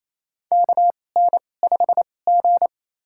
Morse code, heard as a keyed tone, about 27 wpm, spaced as for 20 wpm, 690 Hz. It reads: KN5G
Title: USA Call Signs - 4 Characters - 2x Character Spacing - 28wpm